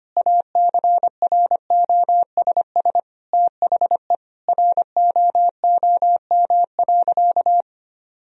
Morse code, heard as a keyed tone, about 25 wpm, 700 Hz. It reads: ACROSS THE ROOM.
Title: Sentences from Top 1000 Words - Review Part 1 - 25wpm